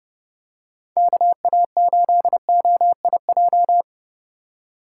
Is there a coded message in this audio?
Yes: Morse code from 1.0 to 3.8 s, heard as a keyed tone, about 30 wpm, 700 Hz. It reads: KA8OIJ